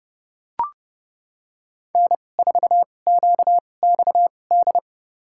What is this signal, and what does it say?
Morse code, heard as a keyed tone, about 30 wpm, spaced as for 22 wpm, 690 Hz. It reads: N4QXD
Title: USA Call Signs - 5 Characters - 2x Character Spacing - 30wpm